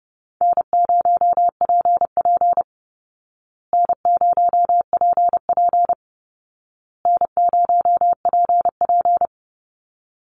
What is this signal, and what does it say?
Morse code, heard as a keyed tone, about 30 wpm, 700 Hz. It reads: N0PP N0PP N0PP
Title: Easy Call Signs Repeated 3x - 30wpm